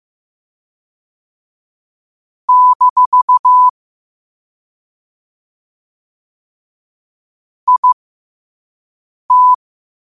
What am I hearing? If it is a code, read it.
Morse code, heard as a keyed tone, about 15 wpm, spaced as for 5 wpm, 1000 Hz. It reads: - IT
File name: qotdfiles / 5wpm